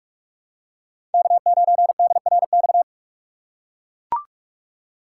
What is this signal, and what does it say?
Morse code, heard as a keyed tone, about 43 wpm, 700 Hz. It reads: K9DRX
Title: USA Call Signs - 5 Characters - 6x Character Spacing - 45wpm